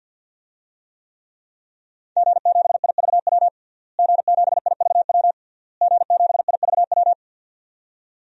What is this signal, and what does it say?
Morse code, heard as a keyed tone, about 49 wpm, 700 Hz. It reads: G7IVW G7IVW G7IVW
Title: DXCC-100 Call Signs - 5 Characters - Repeated 3x - 50wpm